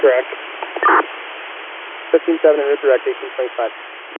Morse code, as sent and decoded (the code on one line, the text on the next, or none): none